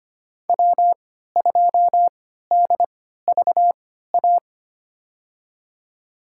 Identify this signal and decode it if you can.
Morse code, heard as a keyed tone, about 25 wpm, spaced as for 14 wpm, 700 Hz. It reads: W2DVA
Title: USA Call Signs - 5 Characters - 3x Character Spacing - 25wpm